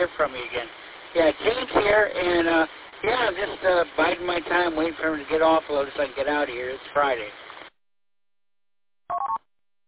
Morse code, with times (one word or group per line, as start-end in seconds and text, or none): none